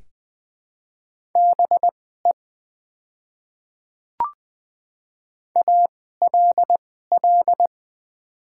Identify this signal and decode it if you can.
Morse code, heard as a keyed tone, about 20 wpm, spaced as for 14 wpm, 700 Hz. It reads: BE ALL
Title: 2x Character Spacing - Top 100 Words - 20wpm